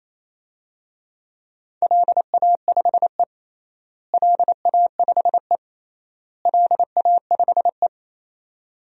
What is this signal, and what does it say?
Morse code, heard as a keyed tone, about 28 wpm, 700 Hz. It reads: LA5E LA5E LA5E
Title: DXCC-100 Call Signs - 4 Characters - Repeated 3x - 28wpm